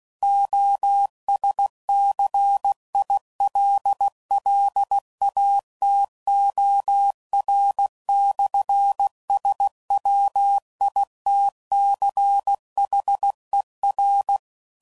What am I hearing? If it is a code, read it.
Morse code, heard as a keyed tone, about 16 wpm, 800 Hz. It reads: OSCILLATOR/SWITCHER
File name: CW Code Groups @15WPM / Technical Subjects / PwrSupply